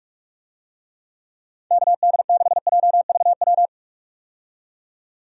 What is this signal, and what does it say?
Morse code, heard as a keyed tone, about 47 wpm, 700 Hz. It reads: KD6JVW
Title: USA Call Signs - 6 Characters - 45wpm